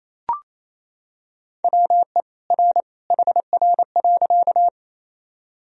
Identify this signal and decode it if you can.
Morse code, heard as a keyed tone, about 28 wpm, 700 Hz. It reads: WE R HR.
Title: Sentences from Top 200 Words and Abbreviations 28wpm